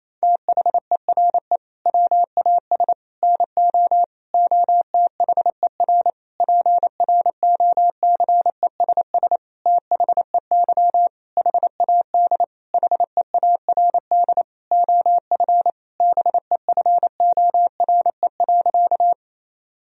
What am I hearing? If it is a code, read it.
Morse code, heard as a keyed tone, about 27 wpm, 700 Hz. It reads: THERE WAS NO OTHER PROCESS THEY HAD HEARD OF BEFORE.
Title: Sentences from Top 1000 Words - Review Part 3 - 28wpm